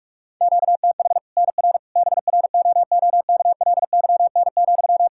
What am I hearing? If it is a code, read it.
Morse code, heard as a keyed tone, about 41 wpm, 700 Hz. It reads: QTH NR BROOKLYN,
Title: Practice QSO - Exchange 1 - Beginning - 45wpm